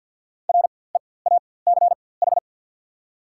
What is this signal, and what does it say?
Morse code, heard as a keyed tone, about 54 wpm, spaced as for 24 wpm, 700 Hz. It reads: REACH
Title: 4x Character Spacing - Top 500 Words - 50wpm